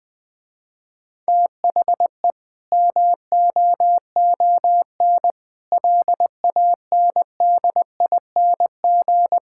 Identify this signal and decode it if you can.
Morse code, heard as a keyed tone, about 20 wpm, 700 Hz. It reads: THE MOON LANDING
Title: Difficult Sentences with Hints - 20wpm